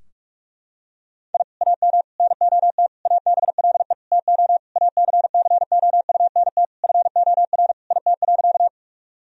Morse code, heard as a keyed tone, about 42 wpm, 700 Hz. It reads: I AM NOT ABLE TO ACCOUNT FOR IT.
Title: Sentences from Top 1000 Words - Review Part 2 - 45wpm